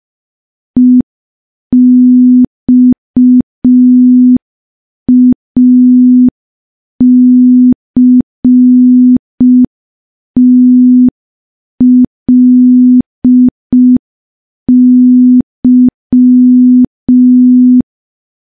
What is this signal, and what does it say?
Morse code, heard as a keyed tone, about 5 wpm, 250 Hz. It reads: EXACTLY
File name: morse / morse-training-05WPM